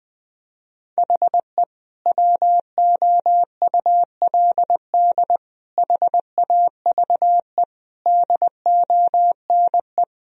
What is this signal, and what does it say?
Morse code, heard as a keyed tone, about 20 wpm, 700 Hz. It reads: HE WOULD HAVE DONE
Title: Sentences from Top 1000 Words - Review Part 1 - 20wpm